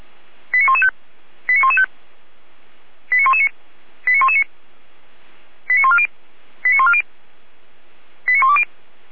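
Five-tone selective call (ZVEI), heard as quick runs of five tones, tones of about 80 ms. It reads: 89115 89115 89119 89119 89140 89140 89120
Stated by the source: ZVEI five-tone selective call received with an SDR.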